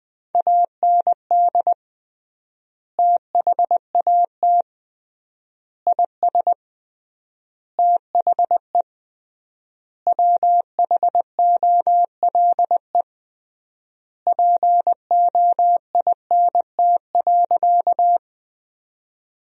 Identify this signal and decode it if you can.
Morse code, heard as a keyed tone, about 20 wpm, 700 Hz. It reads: AND THAT IS THE WHOLE POINT.
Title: 3x Word-Spacing - Sentences from Top 400 Words - 20wpm